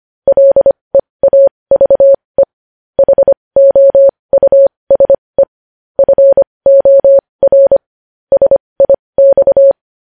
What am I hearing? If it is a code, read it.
Morse code, heard as a keyed tone, about 25 wpm, spaced as for 20 wpm, 550 Hz. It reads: LEAVE HOUSE FOR SIX
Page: News Headlines in Morse Code at 20 WPM